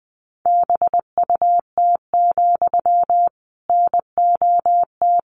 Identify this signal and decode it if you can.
Morse code, heard as a keyed tone, about 20 wpm, 700 Hz. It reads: BUT, NOT
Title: Sentences from Top 600 Words 20wpm